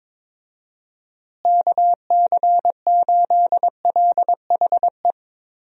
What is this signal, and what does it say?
Morse code, heard as a keyed tone, about 22 wpm, 700 Hz. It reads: KC8LHE
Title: USA Call Signs - 6 Characters - Repeated 2x - 22wpm